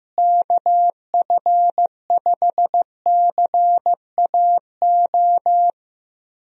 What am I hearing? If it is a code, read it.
Morse code, heard as a keyed tone, about 15 wpm, 700 Hz. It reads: KF5CAO
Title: USA Call Signs - 6 Characters - Repeated 3x - 15wpm